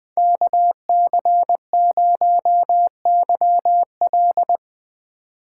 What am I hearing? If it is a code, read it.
Morse code, heard as a keyed tone, about 22 wpm, 700 Hz. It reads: KC0YL
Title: Speed Racing - USA 2x2 Call Signs 20wpm